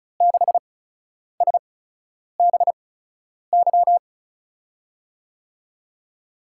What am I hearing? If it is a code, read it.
Morse code, heard as a keyed tone, about 36 wpm, spaced as for 10 wpm, 700 Hz. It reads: 6SBY
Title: USA Call Signs - 5 Characters - 8x Character Spacing - 35wpm